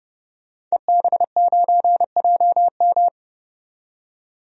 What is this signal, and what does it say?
Morse code, heard as a keyed tone, about 30 wpm, 700 Hz. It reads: EB9JM
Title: DXCC-100 Call Signs - 5 Characters - 30wpm